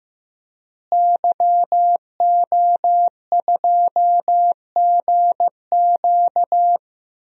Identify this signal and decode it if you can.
Morse code, heard as a keyed tone, about 15 wpm, 700 Hz. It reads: YO2GQ
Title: DXCC-100 Call Signs - 5 Characters - Repeated 2x - 15wpm